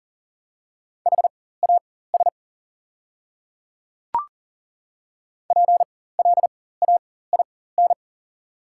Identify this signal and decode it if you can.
Morse code, heard as a keyed tone, about 40 wpm, spaced as for 19 wpm, 690 Hz. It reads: HAS PLAIN
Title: 4x Character Spacing - Top 500 Words - 40wpm